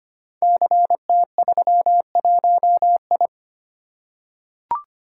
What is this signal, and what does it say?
Morse code, heard as a keyed tone, about 25 wpm, 700 Hz. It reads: CT31I
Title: Code Groups - 5 Letter:Numbers - 25wpm